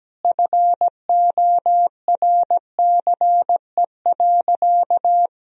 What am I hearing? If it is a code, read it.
Morse code, heard as a keyed tone, about 17 wpm, 700 Hz. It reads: FORCE.
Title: Sentences from Top 1000 Words - Review Part 3 - 17wpm